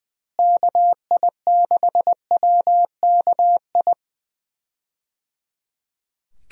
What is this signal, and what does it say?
Morse code, heard as a keyed tone, about 20 wpm, 700 Hz. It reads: KI6WKI
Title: USA Call Signs - 6 Characters - Repeated 3x - 20wpm